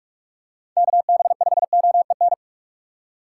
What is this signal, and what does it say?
Morse code, heard as a keyed tone, about 46 wpm, 700 Hz. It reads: KB5OEN
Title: USA Call Signs - 8x Character Spacing - 45wpm